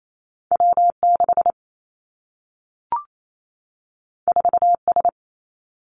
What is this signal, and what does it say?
Morse code, heard as a keyed tone, about 28 wpm, 700 Hz. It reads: W 6 4 S
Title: Two Challenging Letter-Number Combinations 28wpm